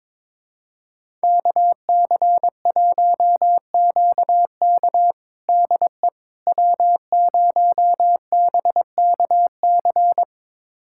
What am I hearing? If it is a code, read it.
Morse code, heard as a keyed tone, about 22 wpm, 700 Hz. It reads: KC1QK DE W0BKC
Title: USA Callsign DE Callsign - 5 Characters - 22wpm